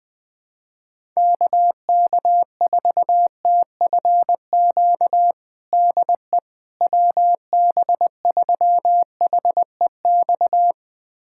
Morse code, heard as a keyed tone, about 20 wpm, 700 Hz. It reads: KK4TFQ DE WB3HEX
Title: USA Callsign DE Callsign - 6 Characters - 20wpm